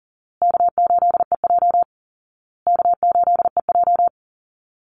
Morse code, heard as a keyed tone, about 40 wpm, 700 Hz. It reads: K8EJ K8EJ
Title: Call Signs Repeated 2x - 40wpm